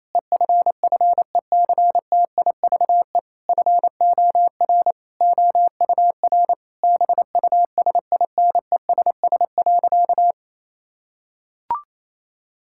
Morse code, heard as a keyed tone, about 28 wpm, 700 Hz. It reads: EFFECTIVE FOR OUR BUSINESS.
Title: Sentences from Top 1000 Words - Review Part 6 - 28wpm